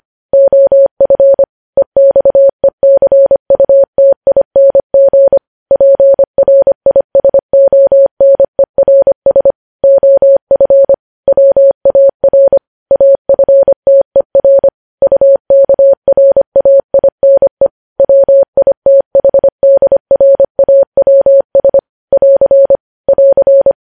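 Morse code, heard as a keyed tone, about 25 wpm, 550 Hz. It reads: OF EXECUTING PRISONERS OF WAR AFTER UKRAINE WITHDRAWS <AR> <AR>
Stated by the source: Morse news headlines at 25 WPM.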